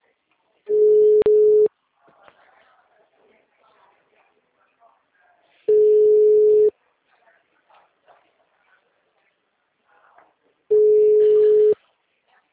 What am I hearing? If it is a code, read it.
Ringback tone (Europe, 425 Hz), with speech.